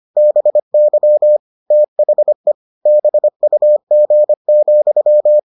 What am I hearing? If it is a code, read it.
Morse code, heard as a keyed tone, about 25 wpm, 600 Hz. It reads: BY THE BUG,